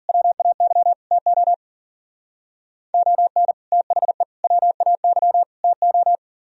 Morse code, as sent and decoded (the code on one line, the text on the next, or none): .-- .- -.-- / - --- / --- -. / - .... / .-- .- -.-- / - ---
WAY TO ON THE WAY TO